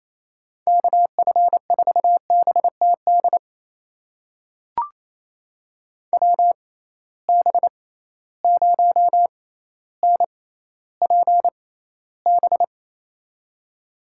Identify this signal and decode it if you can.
Morse code, heard as a keyed tone, about 28 wpm, spaced as for 14 wpm, 700 Hz. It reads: KF4BTD WB0NPB
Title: USA Call Signs - 6 Characters - 6x Character Spacing - 28wpm